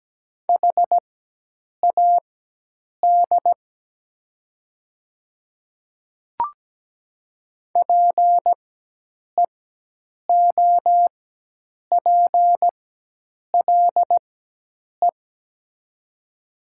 Morse code, heard as a keyed tone, about 17 wpm, spaced as for 8 wpm, 700 Hz. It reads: HAD PEOPLE
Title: 4x Character Spacing - Top 100 Words - 17wpm